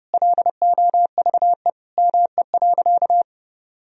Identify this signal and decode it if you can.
Morse code, heard as a keyed tone, about 30 wpm, 700 Hz. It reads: LOVE ME.
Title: Sentences from Top 400 Words and Abbreviations 30wpm